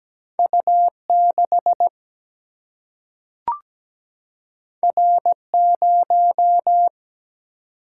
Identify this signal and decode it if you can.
Morse code, heard as a keyed tone, about 17 wpm, 700 Hz. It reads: U6 R0
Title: Single-Letter Single-Number 17wpm